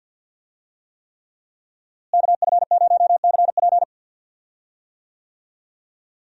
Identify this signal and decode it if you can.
Morse code, heard as a keyed tone, about 51 wpm, 700 Hz. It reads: KF0CP
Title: USA Call Signs - 5 Characters - Repeated 3x - 50wpm